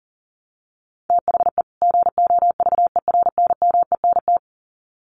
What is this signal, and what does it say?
Morse code, heard as a keyed tone, about 40 wpm, 700 Hz. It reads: THE GOVERNMENT
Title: Sets of 2 Words 40wpm